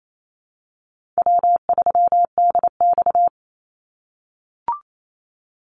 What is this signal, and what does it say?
Morse code, heard as a keyed tone, about 28 wpm, 700 Hz. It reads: W3DX
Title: NA SOTA Activator Callsigns 28wpm